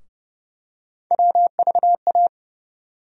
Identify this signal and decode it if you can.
Morse code, heard as a keyed tone, about 30 wpm, 700 Hz. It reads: WVA
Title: USA Call Sign Postfixes - 3 Letters 30wpm